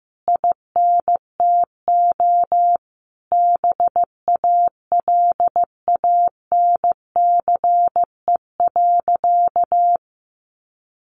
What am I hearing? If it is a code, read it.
Morse code, heard as a keyed tone, about 15 wpm, 700 Hz. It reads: INTO BALANCE.
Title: Sentences from Top 1200 Words 15wpm